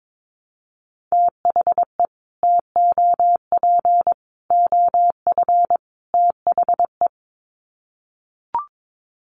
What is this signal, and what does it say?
Morse code, heard as a keyed tone, about 22 wpm, 700 Hz. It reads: THE TOP OF THE
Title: Sets of 4 Words 22wpm